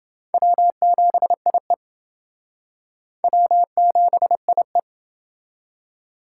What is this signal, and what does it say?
Morse code, heard as a keyed tone, about 28 wpm, 700 Hz. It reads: W7IE W7IE